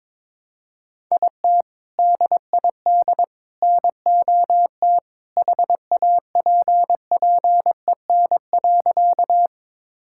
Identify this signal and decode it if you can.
Morse code, heard as a keyed tone, about 22 wpm, 700 Hz. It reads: IT DID NOT HAPPEN.
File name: practice / 22wpm / Sentences with 3-Word Phrases - 22wpm